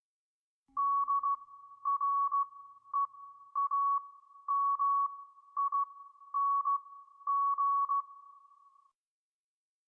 Morse code, heard as a keyed tone, about 16 wpm, spaced as for 11 wpm, 1120 Hz. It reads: DREAMING